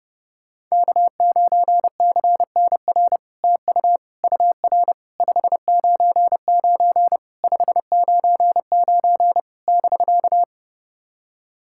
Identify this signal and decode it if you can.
Morse code, heard as a keyed tone, about 28 wpm, 700 Hz. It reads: K9CNR TU UR 599 599 <BK>
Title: SOTA QSO Practice 30wpm